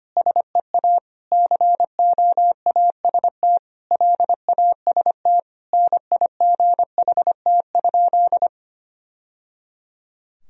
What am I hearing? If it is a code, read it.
Morse code, heard as a keyed tone, about 25 wpm, 700 Hz. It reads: SEA COAST LAST NIGHT?
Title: Sentences from Top 1000 Words - Review Part 4 - 25wpm